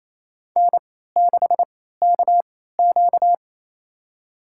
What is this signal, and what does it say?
Morse code, heard as a keyed tone, about 28 wpm, spaced as for 16 wpm, 700 Hz. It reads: N6KQ
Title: USA Call Signs - 4 Characters - 3x Character Spacing - 28wpm